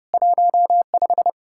Morse code, heard as a keyed tone, about 39 wpm, 700 Hz. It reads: 15